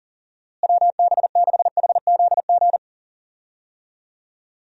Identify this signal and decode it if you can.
Morse code, heard as a keyed tone, about 37 wpm, 700 Hz. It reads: WB6HZG
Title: USA Call Signs - 6 Characters - Repeated 3x - 40wpm